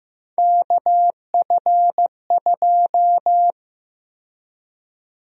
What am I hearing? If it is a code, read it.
Morse code, heard as a keyed tone, about 15 wpm, 700 Hz. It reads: KF2
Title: USA Call Sign Prefixes - Repeated 2x - 15wpm